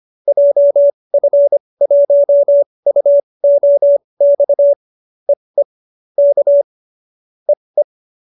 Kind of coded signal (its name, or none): Morse code